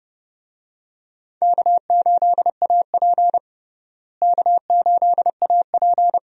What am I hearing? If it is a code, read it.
Morse code, heard as a keyed tone, about 30 wpm, 700 Hz. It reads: K8AP K8AP
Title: USA Call Signs - 4 Characters - Repeated 2x - 30wpm